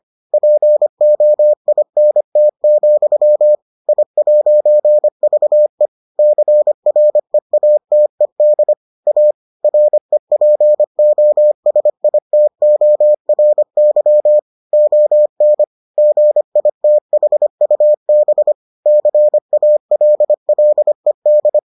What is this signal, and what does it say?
Morse code, heard as a keyed tone, about 25 wpm, 600 Hz. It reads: POINT, I'VE CREATED A REPOSITORY ON GITHUB CALLED